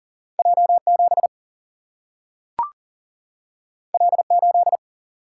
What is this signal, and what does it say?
Morse code, heard as a keyed tone, about 40 wpm, 700 Hz. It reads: J 7 L 8
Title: Single-Letter Single-Number 40wpm